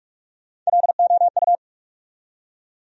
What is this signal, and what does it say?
Morse code, heard as a keyed tone, about 47 wpm, 700 Hz. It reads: LOU
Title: USA Common Names - Max 3 Letters - 45wpm